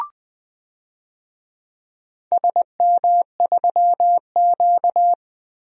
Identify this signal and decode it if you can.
Morse code, heard as a keyed tone, about 20 wpm, 700 Hz. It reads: SM3Q
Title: DXCC-100 Call Signs - 4 Characters - 20wpm